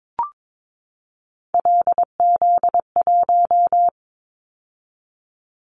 Morse code, heard as a keyed tone, about 22 wpm, 700 Hz. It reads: LZ1
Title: Callsign Prefixes for Top 25 DXCC - 22wpm